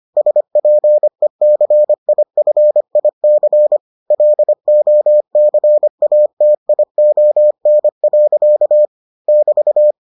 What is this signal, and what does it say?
Morse code, heard as a keyed tone, about 25 wpm, 600 Hz. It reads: SPECIFIC LOCATION. <BT>